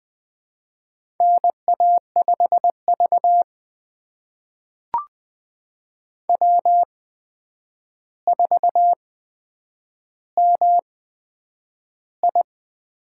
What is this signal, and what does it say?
Morse code, heard as a keyed tone, about 20 wpm, spaced as for 9 wpm, 700 Hz. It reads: NA5V W4MI
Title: USA Call Signs - 4 Characters - 8x Character Spacing - 20wpm